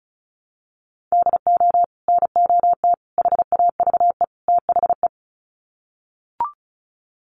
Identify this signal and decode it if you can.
Morse code, heard as a keyed tone, about 34 wpm, 700 Hz. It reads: DO NOT HAVE THE
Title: Sets of 4 Words 35wpm